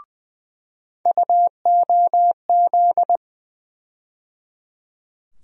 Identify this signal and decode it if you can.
Morse code, heard as a keyed tone, about 21 wpm, 700 Hz. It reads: UOZ
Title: USA Call Sign Postfixes - 3 Letters 20wpm